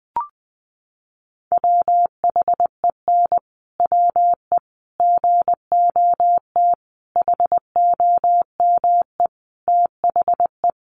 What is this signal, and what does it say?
Morse code, heard as a keyed tone, about 20 wpm, 700 Hz. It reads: WHEN WE GOT HOME THE
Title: Sentences from Top 500 Words 20wpm